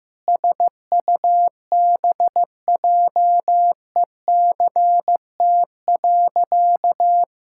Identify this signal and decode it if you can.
Morse code, heard as a keyed tone, about 15 wpm, 700 Hz. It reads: SUBJECT.
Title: Sentences from Top 1600 Words 15wpm